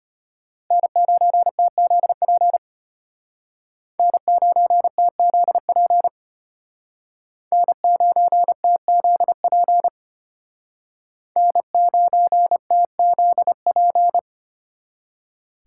Morse code, heard as a keyed tone, about 32 wpm, 700 Hz. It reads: N9TZP N9TZP N9TZP N9TZP